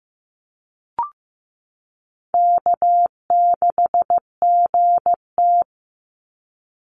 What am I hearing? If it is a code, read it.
Morse code, heard as a keyed tone, about 15 wpm, 700 Hz. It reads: K6GT
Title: Call Signs Repeated 3x - 15wpm